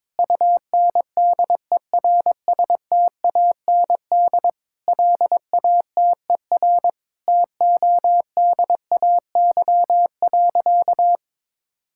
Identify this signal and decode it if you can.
Morse code, heard as a keyed tone, about 22 wpm, 700 Hz. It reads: UNDERSTAND LATER TODAY.
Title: Sentences from Top 1000 Words - Review Part 5 - 22wpm